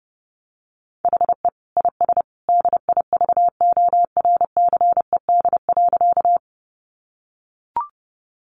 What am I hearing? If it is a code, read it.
Morse code, heard as a keyed tone, about 30 wpm, 700 Hz. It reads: HE IS DIVORCED.
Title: Additional Easy Phrases 30wpm